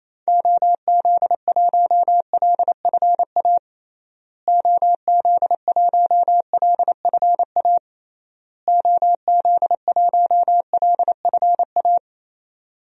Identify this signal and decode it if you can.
Morse code, heard as a keyed tone, about 27 wpm, 700 Hz. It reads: OZ1LFA OZ1LFA OZ1LFA
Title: DXCC-100 Call Signs - 6 Characters - Repeated 3x - 28wpm